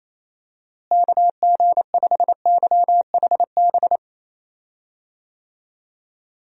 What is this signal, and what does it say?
Morse code, heard as a keyed tone, about 27 wpm, 700 Hz. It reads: KG5YHB
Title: USA Call Signs - 6 Characters - Repeated 2x - 28wpm